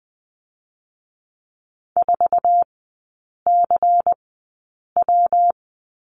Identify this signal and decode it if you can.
Morse code, heard as a keyed tone, about 20 wpm, spaced as for 8 wpm, 700 Hz. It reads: 4CW
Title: 114 Lesson 19e - Letter w - Three Characters 20wpm